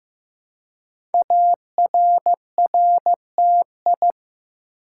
Morse code, heard as a keyed tone, about 15 wpm, 700 Hz. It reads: ARRTI
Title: Code Groups - 5 Letters - 15wpm